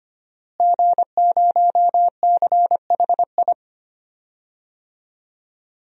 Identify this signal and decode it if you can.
Morse code, heard as a keyed tone, about 25 wpm, 700 Hz. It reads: G0CHI
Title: DXCC-100 Call Signs - 5 Characters - Repeated 2x - 25wpm